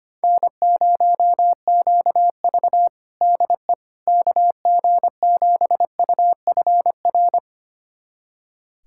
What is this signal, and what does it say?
Morse code, heard as a keyed tone, about 25 wpm, 700 Hz. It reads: N0QV DE KG7UFR
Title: USA Callsign DE Callsign 25wpm